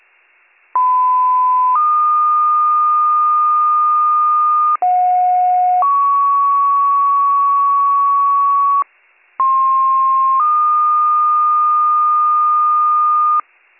Two-tone page, heard as a one-second tone, then a three-second tone, about 1010 then 1220 Hz, about 730 then 1120 Hz, about 1060 then 1220 Hz. It is heard from 0.7 to 13.4 s.